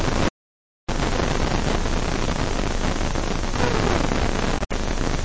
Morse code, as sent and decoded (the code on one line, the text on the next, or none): none